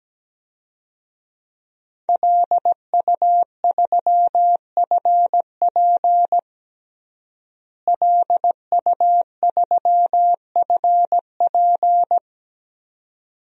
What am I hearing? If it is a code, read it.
Morse code, heard as a keyed tone, about 17 wpm, 700 Hz. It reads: LU3FP LU3FP
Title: DXCC-100 Call Signs - 5 Characters - Repeated 2x - 17wpm